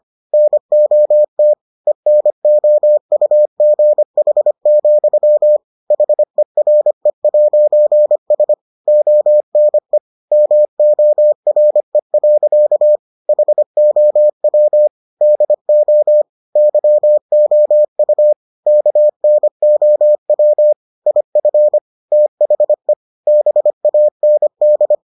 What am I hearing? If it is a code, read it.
Morse code, heard as a keyed tone, about 25 wpm, 600 Hz. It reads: NOT ENOUGH, HERE'S ONE MORE. HOW DO YOU KNOW IF THE BAND